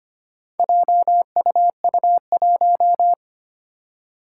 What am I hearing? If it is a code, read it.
Morse code, heard as a keyed tone, about 25 wpm, 700 Hz. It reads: JUU1
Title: Four Challenging Letter-Number Combinations 25wpm